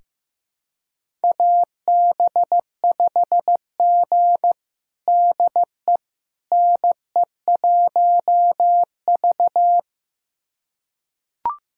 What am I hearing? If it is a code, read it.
Morse code, heard as a keyed tone, about 15 wpm, 700 Hz. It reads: AB5G DE NE1V
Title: USA Callsign DE Callsign - 4 Characters - 15wpm